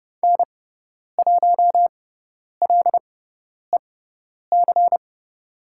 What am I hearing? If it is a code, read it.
Morse code, heard as a keyed tone, about 30 wpm, spaced as for 10 wpm, 700 Hz. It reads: N1LEC
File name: Lesson 18g - Letter m - Call Signs 30wpm